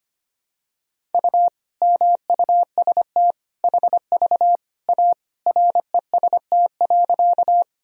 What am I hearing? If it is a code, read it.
Morse code, heard as a keyed tone, about 25 wpm, 700 Hz. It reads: U MUST HV A REST.
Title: Sentences from Top 500 Words and Abbreviations 25wpm